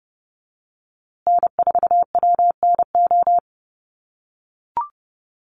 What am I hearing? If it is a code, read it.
Morse code, heard as a keyed tone, about 30 wpm, 700 Hz. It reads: N4WNO
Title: USA Call Signs Repeated 2x 30wpm